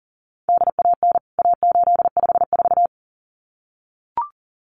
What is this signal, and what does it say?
Morse code, heard as a keyed tone, about 40 wpm, 700 Hz. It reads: DAN A854
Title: CWT Exchange 40wpm